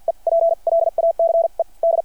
Morse code, heard as a keyed tone, about 53 wpm, spaced as for 38 wpm, 690 Hz. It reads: EPLACED
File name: archived / CRSnet